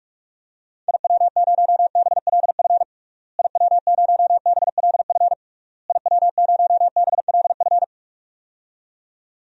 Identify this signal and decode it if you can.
Morse code, heard as a keyed tone, about 46 wpm, 700 Hz. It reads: IW0BLF IW0BLF IW0BLF
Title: DXCC-100 Call Signs - 6 Characters - Repeated 3x - 45wpm